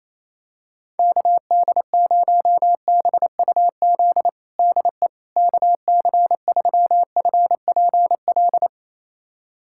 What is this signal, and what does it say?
Morse code, heard as a keyed tone, about 28 wpm, 700 Hz. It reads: KD0BUZ DE KC3FPL
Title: USA Callsign DE Callsign - 6 Characters - 28wpm